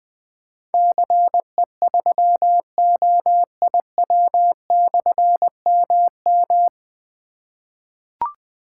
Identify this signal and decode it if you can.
Morse code, heard as a keyed tone, about 21 wpm, 700 Hz. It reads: CE3OIW/MM
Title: Difficult Call Signs 20wpm